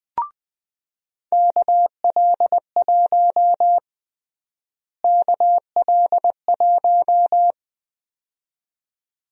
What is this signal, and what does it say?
Morse code, heard as a keyed tone, about 20 wpm, 700 Hz. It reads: KL1 KL1
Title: USA Call Sign Prefixes - Repeated 2x - 20wpm